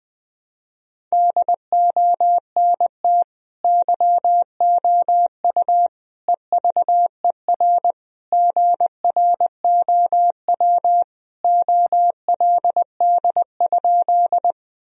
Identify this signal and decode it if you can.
Morse code, heard as a keyed tone, about 20 wpm, 700 Hz. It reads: DONT YOU EVER GROW OLD?
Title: Sentences from Top 1000 Words - Review Part 1 - 20wpm